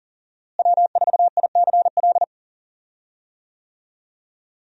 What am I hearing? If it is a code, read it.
Morse code, heard as a keyed tone, about 40 wpm, 700 Hz. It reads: W4ICL
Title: USA Call Signs - 5 Characters - Repeated 3x - 40wpm